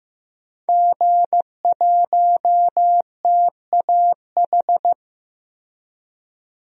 Morse code, heard as a keyed tone, about 15 wpm, 700 Hz. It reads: G1TAH
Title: DXCC-100 Call Signs - 5 Characters - Repeated 3x - 15wpm